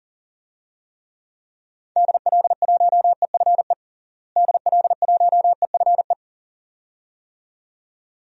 Morse code, heard as a keyed tone, about 40 wpm, 700 Hz. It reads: DL1EFE DL1EFE
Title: DXCC-100 Call Signs - 6 Characters - Repeated 2x - 40wpm